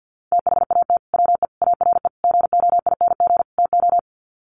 Morse code, heard as a keyed tone, about 48 wpm, 700 Hz. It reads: THAT WE ARE GOING TO
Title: Sets of 5 Words Encore 50wpm